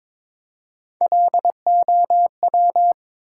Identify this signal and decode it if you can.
Morse code, heard as a keyed tone, about 22 wpm, 700 Hz. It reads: LOW
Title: Three Letter Words - 22wpm